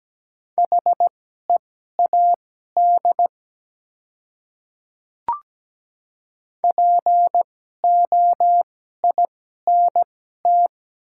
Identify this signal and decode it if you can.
Morse code, heard as a keyed tone, about 17 wpm, spaced as for 12 wpm, 700 Hz. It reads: HEAD POINT
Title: 2x Character Spacing - Top 300 Words - 17wpm